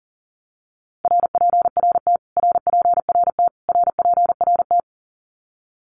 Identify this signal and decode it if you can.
Morse code, heard as a keyed tone, about 40 wpm, 700 Hz. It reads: RPRT RPRT RPRT
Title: Abbreviations Repeated 3x - 40wpm